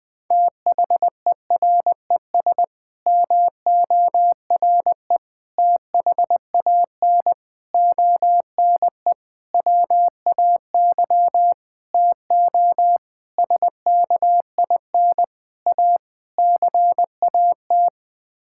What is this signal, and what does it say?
Morse code, heard as a keyed tone, about 19 wpm, 700 Hz. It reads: THERES MORE THAN ONE WAY TO SKIN A CAT
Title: Sets of 9 Words - 20wpm